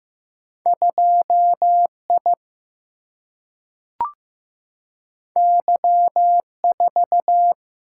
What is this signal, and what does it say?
Morse code, heard as a keyed tone, about 15 wpm, 700 Hz. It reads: 2 I Y 4
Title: Single-Letter Single-Number 15wpm